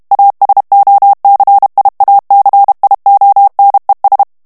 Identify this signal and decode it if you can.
Morse code, heard as a keyed tone, about 32 wpm, 800 Hz. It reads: ASOCIACIONES
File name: morse mp3 / ESPAGNOL TEXTE EN CLAIR 1 VITESSE 30 MOTS MINUTES